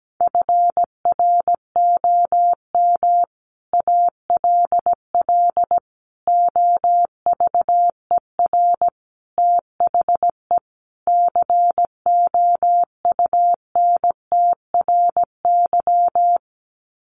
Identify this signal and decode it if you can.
Morse code, heard as a keyed tone, about 17 wpm, 700 Hz. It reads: FROM ALL OVER THE COUNTRY
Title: Sets of 5 Words Encore 17wpm